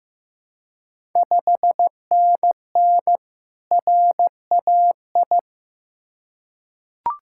Morse code, heard as a keyed tone, about 16 wpm, 700 Hz. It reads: 5NN RAI
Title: Ontario QSO Party Exchange 15wpm